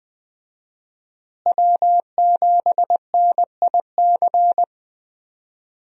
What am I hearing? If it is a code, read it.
Morse code, heard as a keyed tone, about 20 wpm, 700 Hz. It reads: W7NIC
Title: USA Call Signs - 5 Characters - Repeated 3x - 20wpm